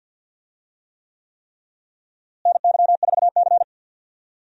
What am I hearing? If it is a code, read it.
Morse code, heard as a keyed tone, about 46 wpm, 700 Hz. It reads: NY4C
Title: USA Call Signs - 4 Characters - Repeated 3x - 50wpm